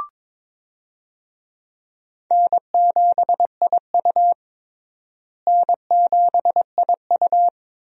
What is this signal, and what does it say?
Morse code, heard as a keyed tone, about 22 wpm, 700 Hz. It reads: N7IU N7IU
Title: USA Call Signs - 4 Characters - Repeated 3x - 22wpm